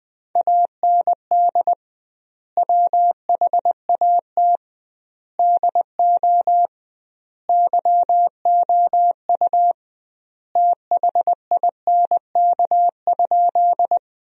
Morse code, heard as a keyed tone, about 20 wpm, 700 Hz. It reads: AND WHAT DO YOU THINK?
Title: 2x Word-Spacing - Sentences from Top 200 Words - 20wpm